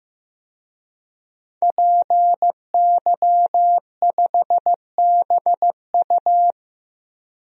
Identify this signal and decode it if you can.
Morse code, heard as a keyed tone, about 15 wpm, 700 Hz. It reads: PY5BU
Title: DXCC-100 Call Signs - 5 Characters - 15wpm